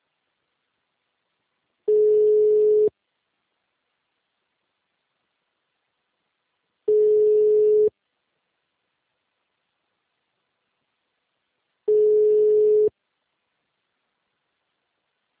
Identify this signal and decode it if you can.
Ringback tone (Europe, 425 Hz).